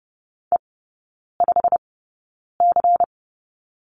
Morse code, heard as a keyed tone, about 30 wpm, spaced as for 9 wpm, 700 Hz. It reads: E5C
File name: Lesson 16e - Letter c - Three Characters 30wpm